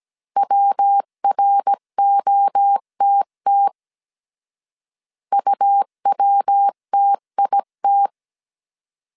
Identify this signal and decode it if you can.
Morse code, heard as a keyed tone, about 17 wpm, 800 Hz. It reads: WROTT UWTIT